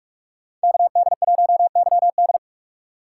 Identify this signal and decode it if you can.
Morse code, heard as a keyed tone, about 44 wpm, 700 Hz. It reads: KD1YD
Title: USA Call Signs - 5 Characters - 6x Character Spacing - 45wpm